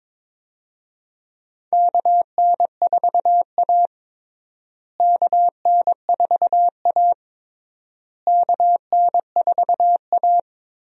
Morse code, heard as a keyed tone, about 22 wpm, 700 Hz. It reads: KN4A KN4A KN4A
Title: USA Call Signs - 4 Characters - Repeated 3x - 22wpm